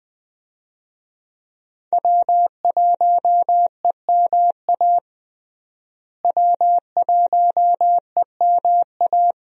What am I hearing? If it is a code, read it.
Morse code, heard as a keyed tone, about 21 wpm, 700 Hz. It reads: W1EMA W1EMA
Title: USA Call Signs - 5 Characters - Repeated 3x - 20wpm